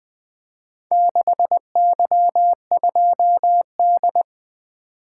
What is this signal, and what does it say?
Morse code, heard as a keyed tone, about 21 wpm, 700 Hz. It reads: 6Y2D
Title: Four Letter-Number Combinations - 20wpm